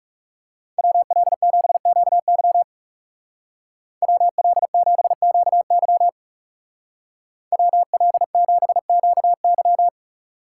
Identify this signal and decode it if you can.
Morse code, heard as a keyed tone, about 39 wpm, 700 Hz. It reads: WL7QY WL7QY WL7QY